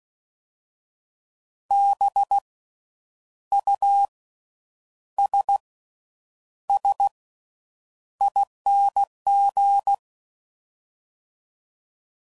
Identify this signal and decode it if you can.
Morse code, heard as a keyed tone, about 16 wpm, 800 Hz. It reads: B U S S ING
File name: Suffixes / Special INGs / Slow With Faster Ings